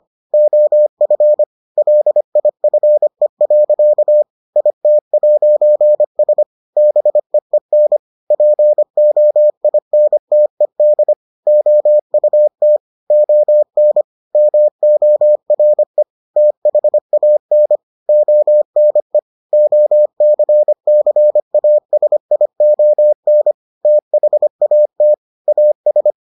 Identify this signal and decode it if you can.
Morse code, heard as a keyed tone, about 25 wpm, 600 Hz. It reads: OF LIFE. IT'S BEEN POINTED OUT ON MORE THAN ONE OCCASION THAT AS